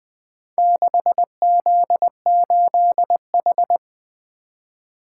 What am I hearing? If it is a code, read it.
Morse code, heard as a keyed tone, about 20 wpm, 700 Hz. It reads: 6Z8H
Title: Four Letter-Number Combinations - 20wpm